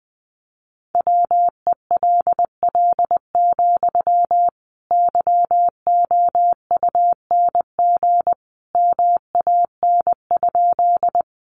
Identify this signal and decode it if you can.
Morse code, heard as a keyed tone, about 20 wpm, 700 Hz. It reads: WELL, YOUNG MAN?
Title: Sentences from Top 400 Words 20wpm